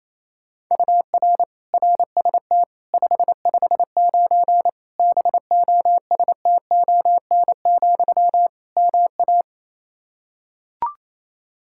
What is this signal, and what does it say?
Morse code, heard as a keyed tone, about 28 wpm, 700 Hz. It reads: UR RST 559 BOSTON, MA
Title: Practice QSO - Exchange 1 - Beginning - 28wpm